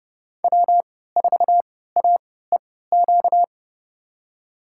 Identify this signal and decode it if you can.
Morse code, heard as a keyed tone, about 30 wpm, spaced as for 17 wpm, 700 Hz. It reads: W4AEQ
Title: USA Call Signs - 3x Character Spacing - 30wpm